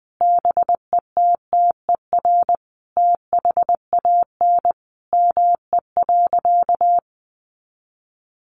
Morse code, heard as a keyed tone, about 20 wpm, 700 Hz. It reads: BETTER THAN ME.
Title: Sentences from Top 400 Words 20wpm